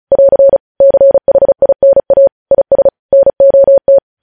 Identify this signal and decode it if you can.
Morse code, heard as a keyed tone, about 36 wpm, 550 Hz. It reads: <AR> CHINA IS NOT
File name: podcast / News headlines-1x-35x35